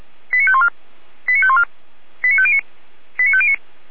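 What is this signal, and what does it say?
Five-tone selective call (ZVEI), heard as quick runs of five tones, tones of about 70 ms. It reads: 89514 89514 89509 89509
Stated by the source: ZVEI five-tone selective call 89001 received by SDR, recorded by Feuerwehr Orlamünde.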